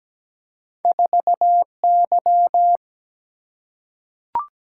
Morse code, heard as a keyed tone, about 17 wpm, 700 Hz. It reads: 4Y
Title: Single-Letter Single-Number 17wpm